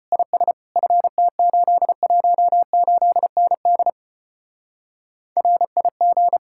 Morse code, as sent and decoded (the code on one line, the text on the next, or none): .. ... / ..-. - ---.. .---- ---.. -. -.. / .-. .. --.
IS FT818ND RIG